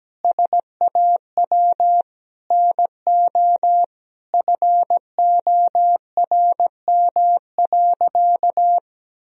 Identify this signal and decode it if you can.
Morse code, heard as a keyed tone, about 17 wpm, 700 Hz. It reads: SAW NO FORM.
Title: Sentences from Top 1000 Words - Review Part 1 - 17wpm